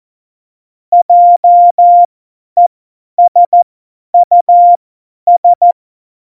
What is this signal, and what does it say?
Morse code, heard as a keyed tone, about 14 wpm, spaced as for 10 wpm, 700 Hz. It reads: JESUS
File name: Morsekurs / Witze / Witz 18. Februar 2020, 50 F70